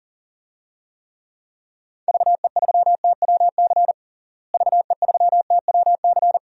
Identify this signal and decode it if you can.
Morse code, heard as a keyed tone, about 40 wpm, 700 Hz. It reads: VE3TWC VE3TWC
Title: DXCC-100 Call Signs - 6 Characters - Repeated 3x - 40wpm